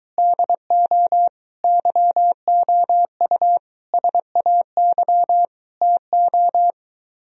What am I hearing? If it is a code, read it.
Morse code, heard as a keyed tone, about 23 wpm, 700 Hz. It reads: DO YOU SAY TO